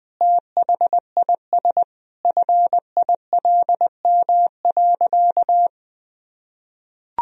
Morse code, heard as a keyed tone, about 20 wpm, 700 Hz. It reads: THIS FILM.
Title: Sentences from Top 1000 Words - Review Part 3 - 20wpm